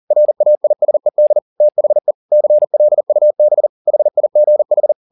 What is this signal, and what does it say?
Morse code, heard as a keyed tone, about 40 wpm, 600 Hz. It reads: RAISED THE CLUB HIGH
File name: the old man and the sea 2 40wpm